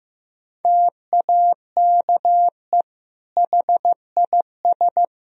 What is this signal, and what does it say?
Morse code, heard as a keyed tone, about 15 wpm, 700 Hz. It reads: TAKE HIS
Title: Sentences from Top 1000 Words - Review Part 3 - 15wpm